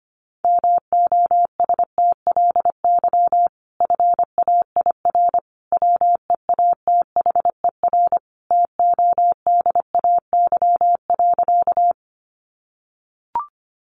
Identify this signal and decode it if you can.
Morse code, heard as a keyed tone, about 25 wpm, 700 Hz. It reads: MOSTLY FAIR WEATHER TODAY.
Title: Sentences from Top 1300 Words 25wpm